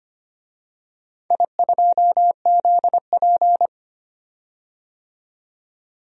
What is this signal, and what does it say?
Morse code, heard as a keyed tone, about 25 wpm, 700 Hz. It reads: I2ZP
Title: DXCC-100 Call Signs - 4 Characters - 25wpm